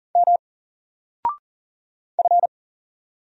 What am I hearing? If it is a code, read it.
Morse code, heard as a keyed tone, about 40 wpm, 700 Hz. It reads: M F
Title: Single Letter-Number 40wpm